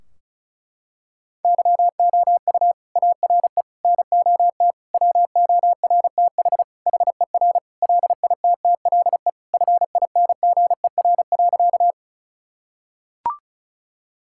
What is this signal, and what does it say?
Morse code, heard as a keyed tone, about 34 wpm, 700 Hz. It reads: YOU ARE NOT WORTH HER LITTLE FINGER.
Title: Sentences from Top 1600 Words 35wpm